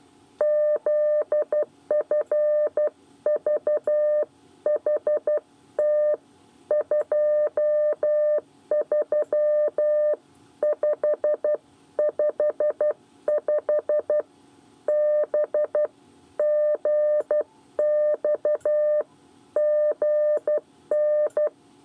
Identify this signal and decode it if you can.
Morse code, heard as a keyed tone, about 12 wpm, 600 Hz. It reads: ZFVHT 23555 BGXGN
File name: 8 bis lezione cw 1